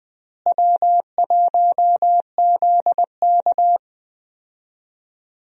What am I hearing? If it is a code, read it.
Morse code, heard as a keyed tone, about 20 wpm, 700 Hz. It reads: W1ZK
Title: USA Call Signs - 4 Characters - Repeated 3x - 20wpm